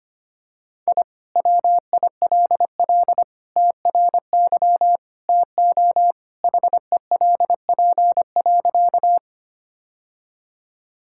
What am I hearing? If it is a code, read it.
Morse code, heard as a keyed tone, about 25 wpm, 700 Hz. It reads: I WILL TRY TO HELP.
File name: practice / 25wpm / Sentences with 3-Word Phrases - 25wpm